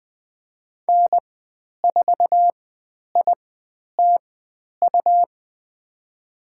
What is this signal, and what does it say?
Morse code, heard as a keyed tone, about 20 wpm, spaced as for 10 wpm, 700 Hz. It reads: N4ITU
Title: 102 Lesson 17g - Letter u - Call Signs 20wpm